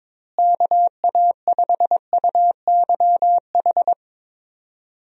Morse code, heard as a keyed tone, about 22 wpm, 700 Hz. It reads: KA5UYH
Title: USA Call Signs - 6 Characters - Repeated 3x - 22wpm